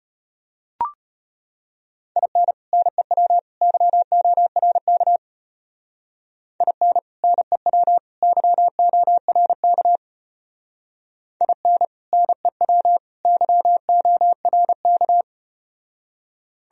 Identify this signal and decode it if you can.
Morse code, heard as a keyed tone, about 34 wpm, 700 Hz. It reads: IN NEW YORK IN NEW YORK IN NEW YORK